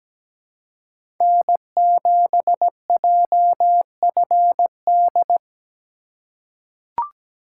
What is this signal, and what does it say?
Morse code, heard as a keyed tone, about 17 wpm, 700 Hz. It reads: N7JFD
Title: USA Call Signs - 5 Characters - 2x Character Spacing - 17wpm